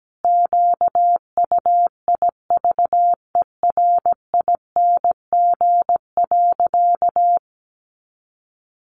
Morse code, heard as a keyed tone, about 17 wpm, 700 Hz. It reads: QUIVERING.
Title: A Sound Of Thunder 17wpm